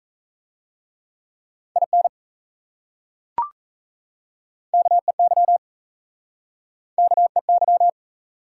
Morse code, heard as a keyed tone, about 36 wpm, 700 Hz. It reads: IN KEY KEY